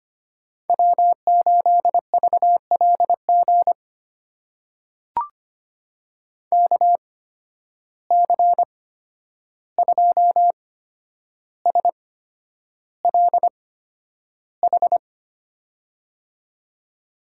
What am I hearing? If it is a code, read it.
Morse code, heard as a keyed tone, about 25 wpm, spaced as for 10 wpm, 700 Hz. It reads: W8VLG KC2SLH
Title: USA Call Signs - 8x Character Spacing - 25wpm